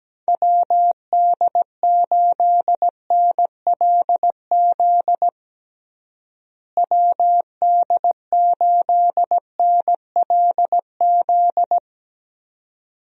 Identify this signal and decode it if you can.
Morse code, heard as a keyed tone, about 17 wpm, 700 Hz. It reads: WD8NLZ WD8NLZ
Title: USA Call Signs - 6 Characters - Repeated 2x - 17wpm